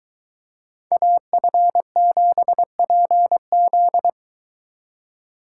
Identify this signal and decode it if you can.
Morse code, heard as a keyed tone, about 22 wpm, 700 Hz. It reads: AF7PZ